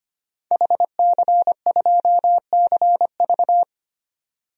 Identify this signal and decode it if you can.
Morse code, heard as a keyed tone, about 25 wpm, 700 Hz. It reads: HC2CV
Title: DXCC-100 Call Signs - 5 Characters - 25wpm